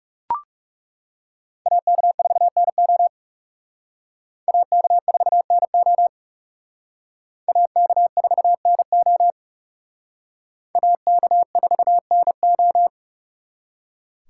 Morse code, heard as a keyed tone, about 39 wpm, 700 Hz. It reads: AK4NO AK4NO AK4NO AK4NO